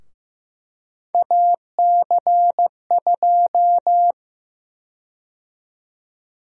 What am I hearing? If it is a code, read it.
Morse code, heard as a keyed tone, about 15 wpm, 700 Hz. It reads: AC2
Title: USA Call Sign Prefixes - Repeated 2x - 15wpm